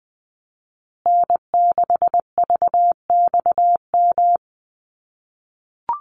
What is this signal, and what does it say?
Morse code, heard as a keyed tone, about 20 wpm, 700 Hz. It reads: N6VXM
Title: USA Call Signs Repeated 3x 20wpm